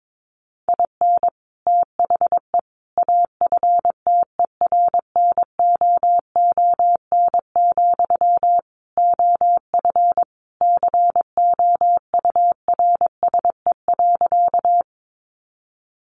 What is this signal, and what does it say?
Morse code, heard as a keyed tone, about 22 wpm, 700 Hz. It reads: IN THE AFTERNOON, OF COURSE.
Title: Sentences from Top 1500 Words 22wpm